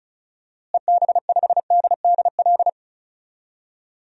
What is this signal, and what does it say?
Morse code, heard as a keyed tone, about 34 wpm, 700 Hz. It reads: EB5DDL
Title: DXCC-100 Call Signs - 6 Characters - Repeated 2x - 35wpm